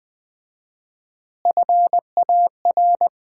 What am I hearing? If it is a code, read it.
Morse code, heard as a keyed tone, about 19 wpm, 700 Hz. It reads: FAR
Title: 4x Word-Spacing - Sets of 4 Words - 20wpm